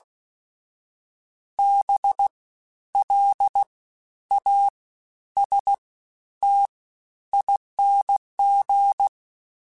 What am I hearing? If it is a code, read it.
Morse code, heard as a keyed tone, about 16 wpm, 800 Hz. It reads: B L A S T ING
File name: Special INGs-mp3s / Special INGs-mp3s / SlowToFaster ing @15